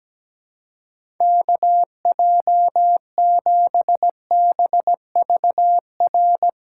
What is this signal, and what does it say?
Morse code, heard as a keyed tone, about 17 wpm, 700 Hz. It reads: KJ7BVR
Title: USA Call Signs - 6 Characters - 2x Character Spacing - 17wpm